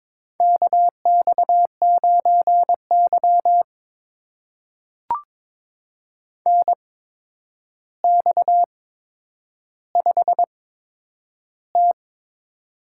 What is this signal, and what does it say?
Morse code, heard as a keyed tone, about 22 wpm, spaced as for 9 wpm, 700 Hz. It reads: KX9Y NX5T
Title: USA Call Signs - 4 Characters - 8x Character Spacing - 22wpm